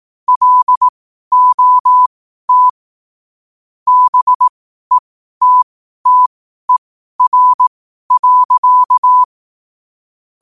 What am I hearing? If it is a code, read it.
Morse code, heard as a keyed tone, about 18 wpm, spaced as for 12 wpm, 1000 Hz. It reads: LOT BETTER.
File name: qotdfiles / 12wpm